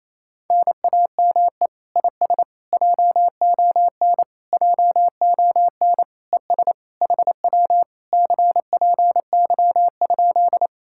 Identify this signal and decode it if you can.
Morse code, heard as a keyed tone, about 27 wpm, 700 Hz. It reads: NAME IS JON JON ES HW CPY?
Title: Speed-Racing Common QSO Phrases Encore - 28wpm